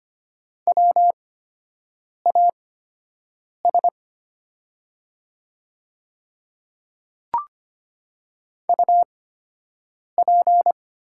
Morse code, heard as a keyed tone, about 25 wpm, spaced as for 7 wpm, 700 Hz. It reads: WAS UP
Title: 8x Character Spacing - Top 200 Words - 25wpm